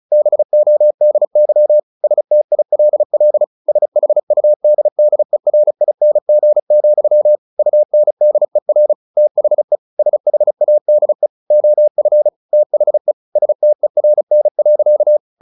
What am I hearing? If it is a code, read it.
Morse code, heard as a keyed tone, about 34 wpm, 600 Hz. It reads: BODY STILL SHUDDERING, UNDER THE SHADE OF THE STERN.
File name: the old man and the sea / the old man and the sea 1 35wpm